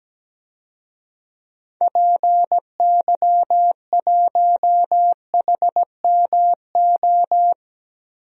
Morse code, heard as a keyed tone, about 17 wpm, 700 Hz. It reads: PY1HMO
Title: DXCC-100 Call Signs - 6 Characters - Repeated 2x - 17wpm